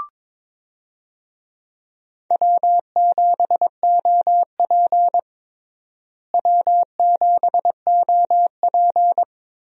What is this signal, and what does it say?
Morse code, heard as a keyed tone, about 22 wpm, 700 Hz. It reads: W7OP W7OP
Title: USA Call Signs - 4 Characters - Repeated 2x - 22wpm